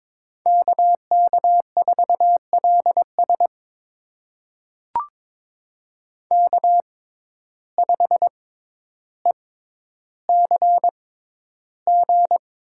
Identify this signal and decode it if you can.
Morse code, heard as a keyed tone, about 22 wpm, spaced as for 11 wpm, 700 Hz. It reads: KK4LS K5ECG
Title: USA Call Signs - 5 Characters - 6x Character Spacing - 22wpm